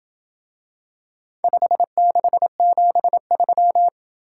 Morse code, heard as a keyed tone, about 27 wpm, 700 Hz. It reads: 5673